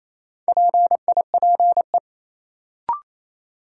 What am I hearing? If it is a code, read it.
Morse code, heard as a keyed tone, about 28 wpm, 700 Hz. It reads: PIPE
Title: Sentences from Top 1500 Words - New Word Drill - 28wpm